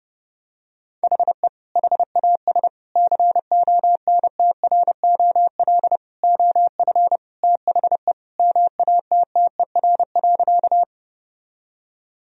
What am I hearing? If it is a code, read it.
Morse code, heard as a keyed tone, about 30 wpm, 700 Hz. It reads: HE HAS CONTROL OF THE MATTER.
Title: Sentences from Top 1000 Words - Review Part 3 - 30wpm